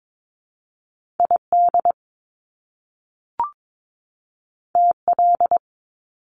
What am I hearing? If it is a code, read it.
Morse code, heard as a keyed tone, about 22 wpm, 700 Hz. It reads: ID TL
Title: Top Two Letter Combinations in Words 22wpm